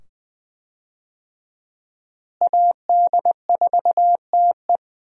Morse code, heard as a keyed tone, about 20 wpm, 700 Hz. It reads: AD4TE
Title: USA Call Signs - 5 Characters - Repeated 2x - 20wpm